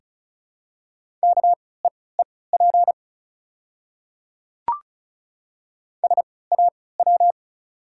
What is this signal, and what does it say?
Morse code, heard as a keyed tone, about 34 wpm, spaced as for 20 wpm, 690 Hz. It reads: KEEP SAW
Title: 3x Character Spacing - Top 500 Words - 35wpm